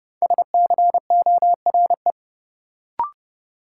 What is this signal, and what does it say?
Morse code, heard as a keyed tone, about 30 wpm, 700 Hz. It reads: SCORE
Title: Sentences from Top 1600 Words - New Word Drill - 30wpm